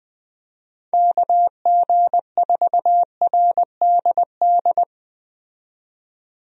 Morse code, heard as a keyed tone, about 21 wpm, 700 Hz. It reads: KG4RDD
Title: USA Call Signs - 6 Characters - 20wpm